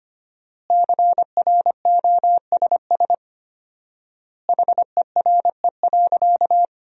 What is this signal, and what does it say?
Morse code, heard as a keyed tone, about 25 wpm, 700 Hz. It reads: CROSS HERE.
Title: 4x Word-Spacing - Sentences from Top 400 Words - 25wpm